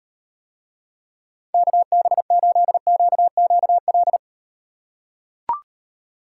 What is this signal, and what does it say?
Morse code, heard as a keyed tone, about 38 wpm, 700 Hz. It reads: KB8QQL